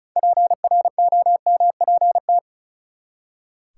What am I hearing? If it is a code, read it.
Morse code, heard as a keyed tone, about 36 wpm, 700 Hz. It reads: PROMPT
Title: Six Letter Words - 35wpm